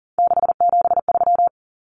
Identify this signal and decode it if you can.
Morse code, heard as a keyed tone, about 40 wpm, 700 Hz. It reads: 673
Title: Common Cut Numbers - 3-Digit 40wpm